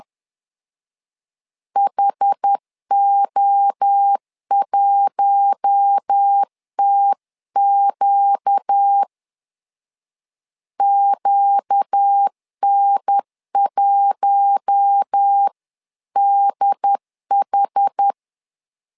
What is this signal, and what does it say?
Morse code, heard as a keyed tone, about 10 wpm, 800 Hz. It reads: HO1TQ QN1DH
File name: LEZIONI / 4 Q1 10wpm 2448